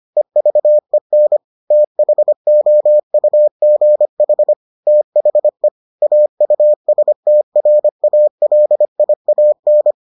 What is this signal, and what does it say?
Morse code, heard as a keyed tone, about 25 wpm, 600 Hz. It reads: EVEN THOUGH THE AUSTRALIAN